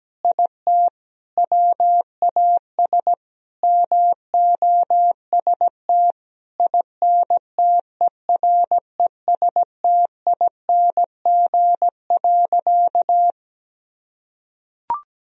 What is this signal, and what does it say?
Morse code, heard as a keyed tone, about 17 wpm, 700 Hz. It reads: IT WAS MOST INTERESTING.
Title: Sentences from Top 1000 Words - Review Part 1 - 17wpm